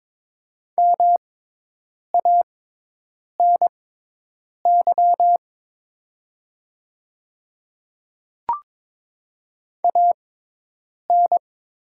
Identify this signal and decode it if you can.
Morse code, heard as a keyed tone, about 22 wpm, spaced as for 8 wpm, 700 Hz. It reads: MANY AN
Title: 6x Character Spacing - Top 200 Words - 22wpm